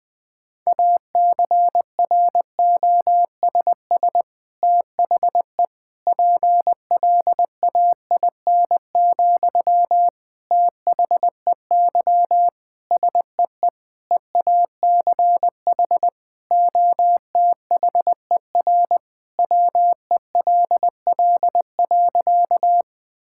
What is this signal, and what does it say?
Morse code, heard as a keyed tone, about 21 wpm, 700 Hz. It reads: ACROSS THE PLAIN, THEY SEE EACH OTHER WELL.
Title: Sentences from Top 1000 Words - Review Part 5 - 20wpm